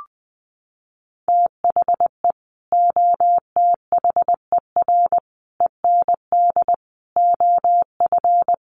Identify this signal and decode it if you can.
Morse code, heard as a keyed tone, about 20 wpm, 700 Hz. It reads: THE OTHER END OF
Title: Sets of 5 Words 20wpm